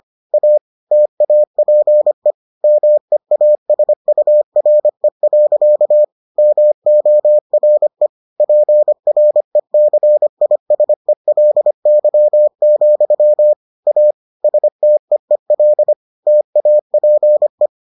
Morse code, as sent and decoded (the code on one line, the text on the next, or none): .- / - .- .--. / -- . .- ... ..- .-. .-.-.- / -- --- .-. / .--. .-. -.-. .. ... .-.. -.-- --..-- / .- / ... - . .-.. / - .- .--. .
A TAPE MEASURE. MORE PRECISELY, A STEEL TAPE